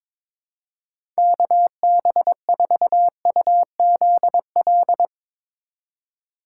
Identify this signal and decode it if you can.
Morse code, heard as a keyed tone, about 22 wpm, 700 Hz. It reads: KB4UZL
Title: USA Call Signs - 3x Character Spacing - 22wpm